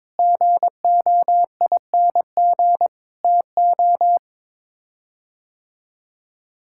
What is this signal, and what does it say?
Morse code, heard as a keyed tone, about 22 wpm, 700 Hz. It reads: GOING TO
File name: practice / 22wpm / Sets of 8 Words - 22wpm